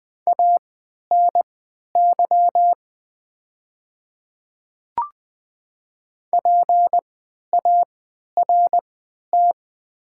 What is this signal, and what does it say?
Morse code, heard as a keyed tone, about 20 wpm, spaced as for 11 wpm, 700 Hz. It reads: ANY PART
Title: 3x Character Spacing - Top 200 Words - 20wpm